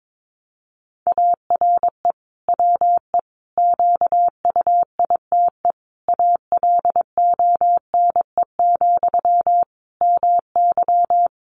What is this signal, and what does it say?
Morse code, heard as a keyed tone, about 22 wpm, 700 Hz. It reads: ARE WE QUITE ALONE, MY
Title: Sentences from Top 1000 Words 22wpm